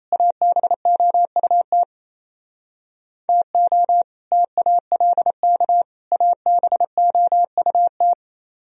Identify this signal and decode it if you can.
Morse code, heard as a keyed tone, about 33 wpm, 700 Hz. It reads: ABOUT TO TALK ABOUT
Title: Speed-Racing - Sets of 3 Words 28wpm